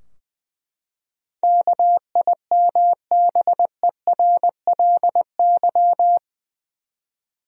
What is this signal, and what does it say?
Morse code, heard as a keyed tone, about 20 wpm, 700 Hz. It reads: KIMBERLY
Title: USA Common Names - 20wpm